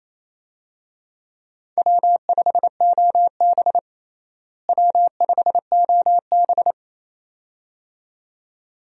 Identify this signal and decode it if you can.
Morse code, heard as a keyed tone, about 28 wpm, 700 Hz. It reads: W5OB W5OB
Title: USA Call Signs - 4 Characters - Repeated 2x - 28wpm